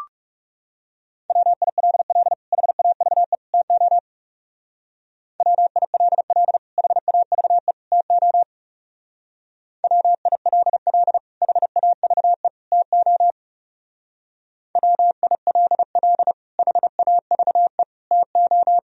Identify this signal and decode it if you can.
Morse code, heard as a keyed tone, about 35 wpm, 700 Hz. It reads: WILL HAVE TO WILL HAVE TO WILL HAVE TO WILL HAVE TO
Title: Speed-Racing - Sets of 3 Words 30wpm